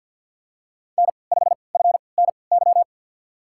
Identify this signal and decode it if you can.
Morse code, heard as a keyed tone, about 47 wpm, spaced as for 28 wpm, 700 Hz. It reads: N5FNY
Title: USA Call Signs - 5 Characters - 3x Character Spacing - 50wpm